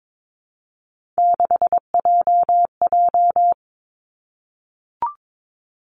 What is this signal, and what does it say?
Morse code, heard as a keyed tone, about 22 wpm, 700 Hz. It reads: 6JJ
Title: Three Challenging Letter-Number Combinations 22wpm